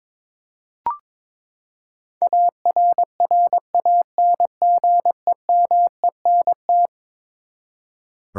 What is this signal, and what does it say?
Morse code, heard as a keyed tone, about 22 wpm, 700 Hz. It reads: ARRANGEMENT
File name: practice / 22wpm / Eleven Letter Words - 22wpm